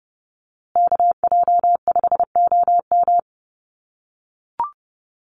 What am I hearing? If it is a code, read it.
Morse code, heard as a keyed tone, about 30 wpm, 700 Hz. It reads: KJ5OM
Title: USA Call Signs Repeated 2x 30wpm